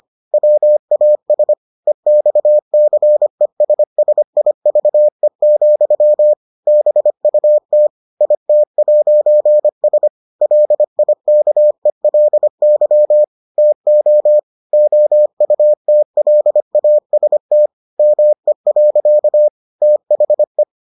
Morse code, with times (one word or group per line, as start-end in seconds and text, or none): 0.3-1.5 WAS
1.9-6.3 EXCESSIVE,
6.7-7.9 BUT
8.2-10.1 IT'S
10.4-13.2 LIKELY
13.6-14.4 TO
14.7-17.7 OUTLAST
18.0-19.5 ME.
19.8-20.6 THE